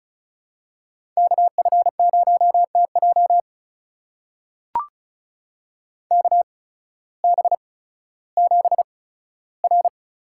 Morse code, heard as a keyed tone, about 36 wpm, spaced as for 16 wpm, 700 Hz. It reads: KF0TJ KB7R
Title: USA Call Signs - 5 Characters - 8x Character Spacing - 35wpm